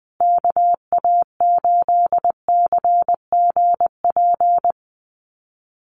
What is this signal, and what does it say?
Morse code, heard as a keyed tone, about 20 wpm, 700 Hz. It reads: KA8CGP
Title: USA Call Signs Repeated 2x 20wpm